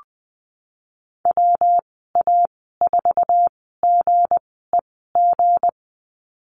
Morse code, heard as a keyed tone, about 20 wpm, spaced as for 15 wpm, 700 Hz. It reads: WA4GEG
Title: 167 Lesson 27g - Number 7 - Call Signs 20wpm